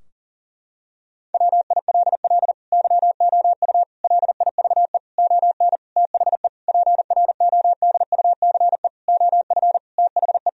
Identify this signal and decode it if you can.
Morse code, heard as a keyed tone, about 40 wpm, 700 Hz. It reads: WILL YOU LIVE ON THE PRODUCE OF THE
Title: Sentences from Top 1000 Words - Review Part 3 - 40wpm